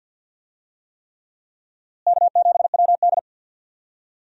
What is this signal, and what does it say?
Morse code, heard as a keyed tone, about 50 wpm, 700 Hz. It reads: K7WD
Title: USA Call Signs - 4 Characters - Repeated 3x - 50wpm